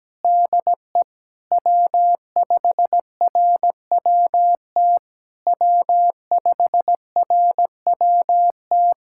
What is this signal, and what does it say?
Morse code, heard as a keyed tone, about 17 wpm, 700 Hz. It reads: DE W5RWT W5RWT
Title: SOTA QSO Practice 17wpm